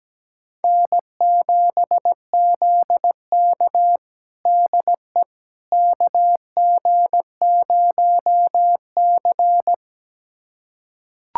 Morse code, heard as a keyed tone, about 17 wpm, 700 Hz. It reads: N7ZK DE KG0C
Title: USA Callsign DE Callsign - 4 Characters - 17wpm